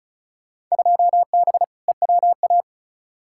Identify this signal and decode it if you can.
Morse code, heard as a keyed tone, about 36 wpm, 700 Hz. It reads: 2B EWA
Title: Field Day Exchanges - 35wpm